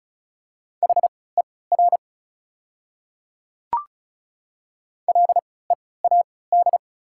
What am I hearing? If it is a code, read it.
Morse code, heard as a keyed tone, about 34 wpm, spaced as for 20 wpm, 690 Hz. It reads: HER LEAD
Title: 3x Character Spacing - Top 500 Words - 35wpm